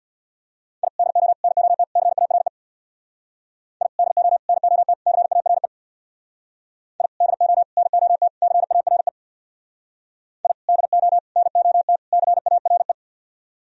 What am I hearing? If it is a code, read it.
Morse code, heard as a keyed tone, about 61 wpm, 700 Hz. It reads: I DO NOT CARE I DO NOT CARE I DO NOT CARE I DO NOT CARE
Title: Speed-Racing - Sets of 4 Words 50wpm